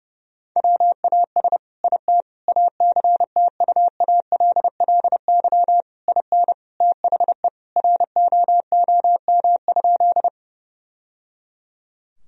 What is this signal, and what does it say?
Morse code, heard as a keyed tone, about 30 wpm, 700 Hz. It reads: WAS IT ACTUALLY IN THE ROOM?
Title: Sentences from Top 1000 Words - Review Part 4 - 30wpm